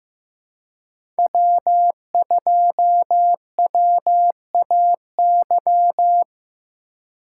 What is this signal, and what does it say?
Morse code, heard as a keyed tone, about 15 wpm, 700 Hz. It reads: W2WAY
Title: USA Call Signs - 5 Characters - 3x Character Spacing - 15wpm